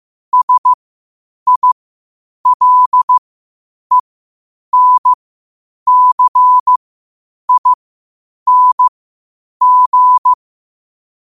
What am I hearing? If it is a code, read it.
Morse code, heard as a keyed tone, about 15 wpm, spaced as for 8 wpm, 1000 Hz. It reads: SILENCING
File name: qotdfiles / 8wpm